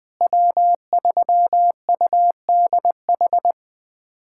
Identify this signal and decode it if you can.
Morse code, heard as a keyed tone, about 20 wpm, 700 Hz. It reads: W3UDH
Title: USA Callsign DE Callsign - 5 Characters - 20wpm